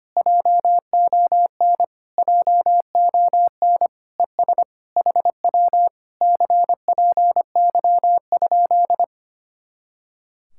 Morse code, heard as a keyed tone, about 24 wpm, 700 Hz. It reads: JON JON ES HW CPY?
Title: Speed-Racing Common QSO Phrases - 25wpm